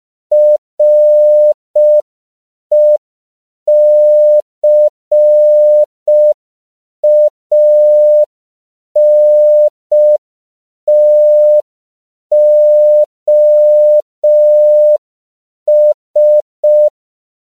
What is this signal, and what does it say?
Morse code, heard as a keyed tone, about 5 wpm, 600 Hz. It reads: RECANTOS
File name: CW / 5PPM